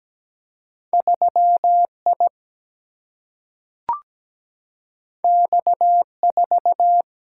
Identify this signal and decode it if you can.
Morse code, heard as a keyed tone, about 17 wpm, 700 Hz. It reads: 3I X4
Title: Single-Letter Single-Number 17wpm